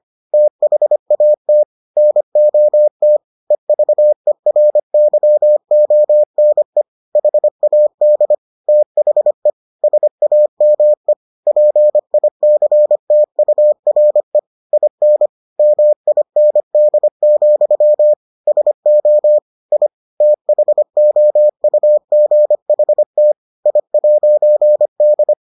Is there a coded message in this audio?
Yes: Morse code from 0.3 to 25.3 s, heard as a keyed tone, about 25 wpm, 600 Hz. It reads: THAT NOT EVERYONE HAD THE SAME PICTURE IN MIND, SO I THOUGHT I'D